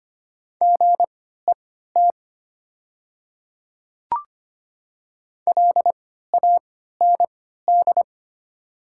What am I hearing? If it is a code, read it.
Morse code, heard as a keyed tone, about 25 wpm, spaced as for 14 wpm, 690 Hz. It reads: GET LAND
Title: 3x Character Spacing - Top 200 Words - 25wpm